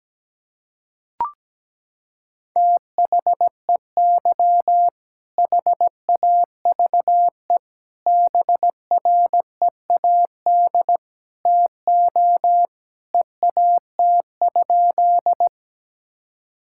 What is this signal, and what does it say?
Morse code, heard as a keyed tone, about 17 wpm, 700 Hz. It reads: THEY HAVE BREAD TO EAT?
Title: Sentences from Top 1600 Words 17wpm